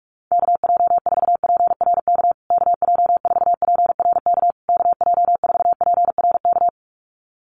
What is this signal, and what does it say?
Morse code, heard as a keyed tone, about 44 wpm, 700 Hz. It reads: KJ4PRK KJ4PRK KJ4PRK
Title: USA Call Signs Repeated 3x 45wpm